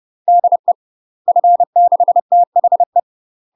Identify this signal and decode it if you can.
Morse code, heard as a keyed tone, about 30 wpm, 700 Hz. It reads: DE F6THE
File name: UK MASTER QSOs TRAINING / 30WPM / UK MASTER QSOs 0020